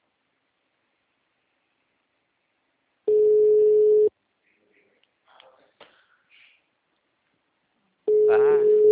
Ringback tone (Europe, 425 Hz).